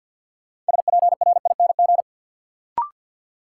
Morse code, heard as a keyed tone, about 51 wpm, 700 Hz. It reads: SPRING
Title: Sentences from Top 1200 Words - New Word Drill - 50wpm